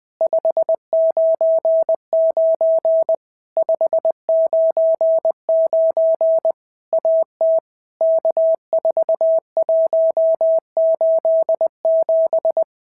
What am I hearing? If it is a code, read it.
Morse code, heard as a keyed tone, about 20 wpm, 650 Hz. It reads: 599 599 AT K4187
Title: POTA QSO Practice - 20wpm